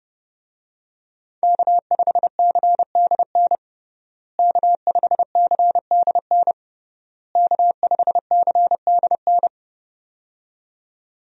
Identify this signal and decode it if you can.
Morse code, heard as a keyed tone, about 30 wpm, 700 Hz. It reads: K5CDN K5CDN K5CDN
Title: USA Call Signs - 5 Characters - Repeated 3x - 30wpm